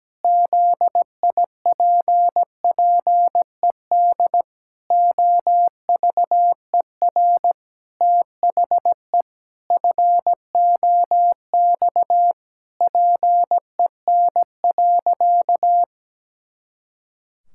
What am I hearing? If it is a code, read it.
Morse code, heard as a keyed tone, about 17 wpm, 700 Hz. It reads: ZIPPED OVER THE FOX PEN.
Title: Pangram Challenge v2 - 17wpm